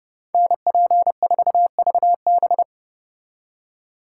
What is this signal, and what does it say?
Morse code, heard as a keyed tone, about 37 wpm, 700 Hz. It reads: NP4VB
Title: Speed Racing - USA 2x2 Call Signs 30wpm